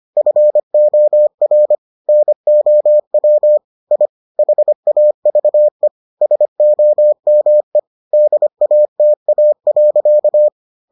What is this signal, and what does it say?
Morse code, heard as a keyed tone, about 25 wpm, 600 Hz. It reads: FOR NOW I HAVE SOME DATA.